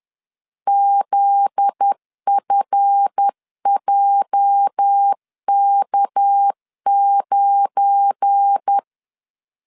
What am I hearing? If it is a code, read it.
Morse code, heard as a keyed tone, about 10 wpm, 800 Hz. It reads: ZFJK9
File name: LEZIONI / 8 YJ45 / 8 YJ 10wpm 2651